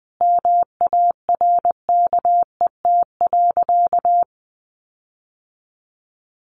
Morse code, heard as a keyed tone, about 20 wpm, 700 Hz. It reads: MARKET.
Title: Sentences from Top 1400 Words 20wpm